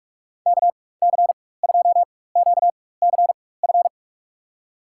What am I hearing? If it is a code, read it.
Morse code, heard as a keyed tone, about 43 wpm, spaced as for 21 wpm, 700 Hz. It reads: KC2QCF
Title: USA Call Signs - 6 Characters - 4x Character Spacing - 45wpm